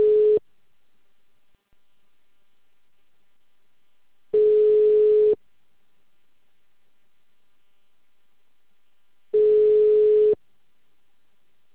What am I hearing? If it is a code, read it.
Ringback tone (Europe, 425 Hz).
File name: out- -2044- - - .22654